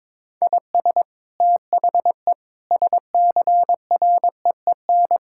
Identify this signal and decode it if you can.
Morse code, heard as a keyed tone, about 22 wpm, 700 Hz. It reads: IS THE SCREEN
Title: Sentences from Top 1000 Words - Review Part 6 - 22wpm